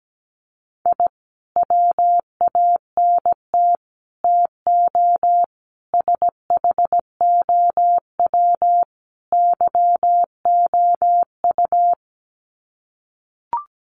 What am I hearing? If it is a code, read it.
Morse code, heard as a keyed tone, about 17 wpm, 700 Hz. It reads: I WANT TO SHOW YOU
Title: Sets of 5 Words 17wpm